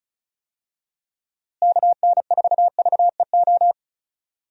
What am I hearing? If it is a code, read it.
Morse code, heard as a keyed tone, about 36 wpm, 700 Hz. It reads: KN4VEO
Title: USA Call Signs - 6 Characters - Repeated 3x - 35wpm